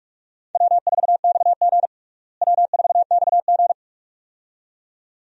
Morse code, heard as a keyed tone, about 42 wpm, 700 Hz. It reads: W4XG W4XG
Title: USA Call Signs - 4 Characters - Repeated 2x - 45wpm